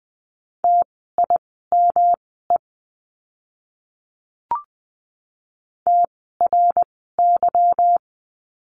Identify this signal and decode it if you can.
Morse code, heard as a keyed tone, about 20 wpm, spaced as for 15 wpm, 700 Hz. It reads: TIME TRY
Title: 160 Lesson 26f - Letter g - Words 20wpm